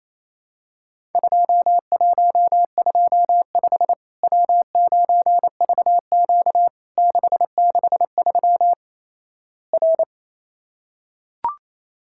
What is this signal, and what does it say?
Morse code, heard as a keyed tone, about 28 wpm, 700 Hz. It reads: 2125 W9VQ 663 R
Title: Worked All Europe DX Contest - QTC Exchange - 28wpm